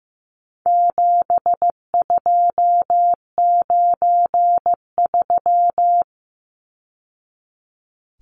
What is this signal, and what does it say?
Morse code, heard as a keyed tone, about 15 wpm, 700 Hz. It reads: 7293
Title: Numbers - 4-Digit 15wpm